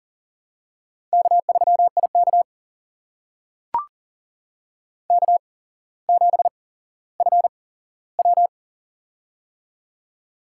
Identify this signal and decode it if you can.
Morse code, heard as a keyed tone, about 38 wpm, spaced as for 17 wpm, 700 Hz. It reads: K3IK K7FW
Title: USA Call Signs - 4 Characters - 8x Character Spacing - 40wpm